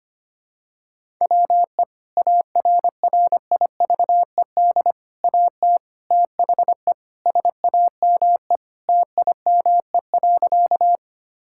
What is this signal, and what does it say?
Morse code, heard as a keyed tone, about 25 wpm, 700 Hz. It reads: WE ARRIVED AT THE SAME TIME.
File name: practice / 25wpm / Sentences with 4-Word Phrases - 25wpm